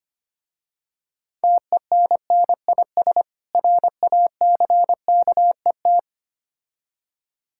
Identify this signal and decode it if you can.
Morse code, heard as a keyed tone, about 25 wpm, 700 Hz. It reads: TENNIS RACKET
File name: practice / 25wpm / Difficult Words with Hints - 25wpm